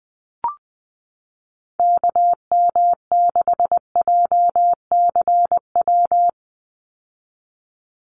Morse code, heard as a keyed tone, about 19 wpm, 700 Hz. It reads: KM6JCW
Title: USA Call Signs 20wpm